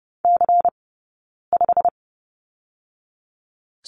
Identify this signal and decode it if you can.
Morse code, heard as a keyed tone, about 30 wpm, 700 Hz. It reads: C 5
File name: course / 30:10wpm / 099 Lesson 17d - Letter u - Two Characters 30wpm